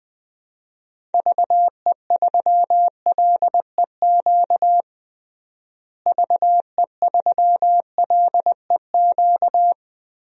Morse code, heard as a keyed tone, about 20 wpm, 700 Hz. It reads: VE3LEQ VE3LEQ
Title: DXCC-100 Call Signs - 6 Characters - Repeated 3x - 20wpm